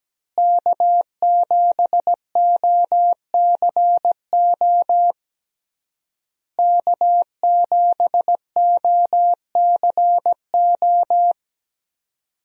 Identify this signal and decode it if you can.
Morse code, heard as a keyed tone, about 17 wpm, 700 Hz. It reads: K7OCO K7OCO
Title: USA Call Signs - 5 Characters - Repeated 3x - 17wpm